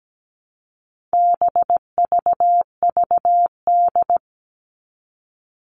Morse code, heard as a keyed tone, about 17 wpm, 700 Hz. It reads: BVVD
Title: All Cut Numbers - 4-Digit 17wpm